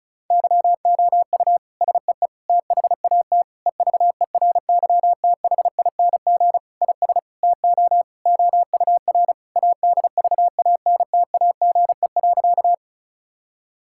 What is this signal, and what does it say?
Morse code, heard as a keyed tone, about 36 wpm, 700 Hz. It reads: YOU SEE THAT EVERYTHING IS TO OUR ADVANTAGE.
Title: Sentences from Top 1000 Words - Review Part 6 - 35wpm